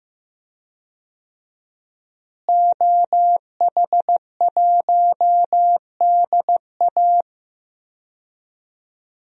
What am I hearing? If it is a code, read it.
Morse code, heard as a keyed tone, about 15 wpm, 700 Hz. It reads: OH1DA
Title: DXCC-100 Call Signs - 5 Characters - Repeated 2x - 15wpm